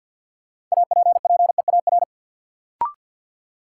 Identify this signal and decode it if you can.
Morse code, heard as a keyed tone, about 48 wpm, 700 Hz. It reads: APPEAR
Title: Sentences from Top 1000 Words - Review Part 2 - New Word Drill - 50wpm